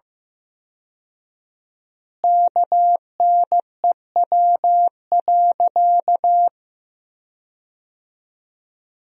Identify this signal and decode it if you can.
Morse code, heard as a keyed tone, about 15 wpm, 700 Hz. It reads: KNEW.
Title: 4x Word-Spacing - Sentences from Top 400 Words - 15wpm